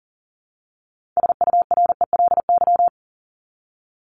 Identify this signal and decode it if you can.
Morse code, heard as a keyed tone, about 40 wpm, 700 Hz. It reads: SURELY
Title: LY Suffix - Top 100 Words with Base Words 40wpm